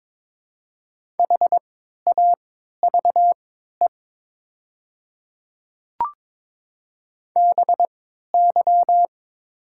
Morse code, heard as a keyed tone, about 22 wpm, spaced as for 12 wpm, 700 Hz. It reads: HAVE BY
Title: 3x Character Spacing - Top 100 Words - 22wpm